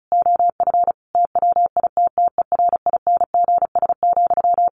Morse code, heard as a keyed tone, about 34 wpm, 700 Hz. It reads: OF TWITTERINGS,
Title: A Sound Of Thunder 35wpm